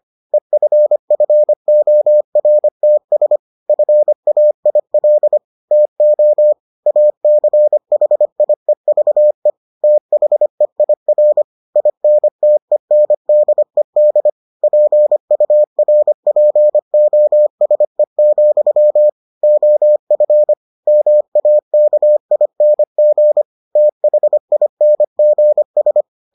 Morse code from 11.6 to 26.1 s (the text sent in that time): INTENDED PURPOSE, OF MAKING THINGS